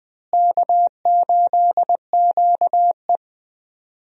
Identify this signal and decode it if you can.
Morse code, heard as a keyed tone, about 21 wpm, 700 Hz. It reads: K8QE
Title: USA Call Signs - 4 Characters - Repeated 2x - 20wpm